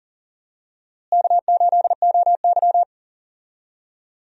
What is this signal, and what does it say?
Morse code, heard as a keyed tone, about 40 wpm, 700 Hz. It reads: K8OY
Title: USA Call Signs - 4 Characters - Repeated 3x - 40wpm